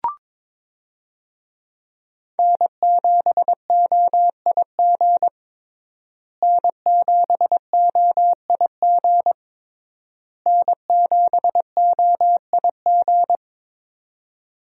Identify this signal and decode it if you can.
Morse code, heard as a keyed tone, about 22 wpm, 700 Hz. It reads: N7OIG N7OIG N7OIG
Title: USA Call Signs - 5 Characters - Repeated 3x - 22wpm